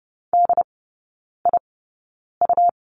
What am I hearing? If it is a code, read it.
Morse code, heard as a keyed tone, about 30 wpm, spaced as for 9 wpm, 700 Hz. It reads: DIU
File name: Lesson 17e - Letter u - Three Characters 30wpm